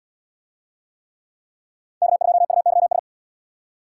Morse code, heard as a keyed tone, about 70 wpm, 700 Hz. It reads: K2UYS